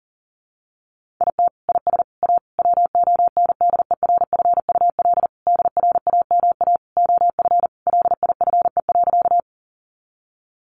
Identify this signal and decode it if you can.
Morse code, heard as a keyed tone, about 40 wpm, 700 Hz. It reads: IT IS A WONDERFUL DRAMA OF LIFE.
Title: Sentences from Top 1500 Words 40wpm